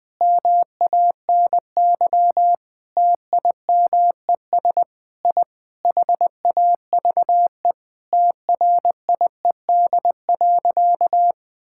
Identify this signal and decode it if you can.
Morse code, heard as a keyed tone, about 19 wpm, 700 Hz. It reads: MANY TIMES I HAVE TRIED.
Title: Sentences with 7-Word Phrases - Spoken Beginning - 20wpm